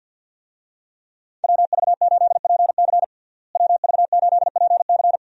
Morse code, heard as a keyed tone, about 51 wpm, 700 Hz. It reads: WV8PC WV8PC
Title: USA Call Signs - 5 Characters - Repeated 3x - 50wpm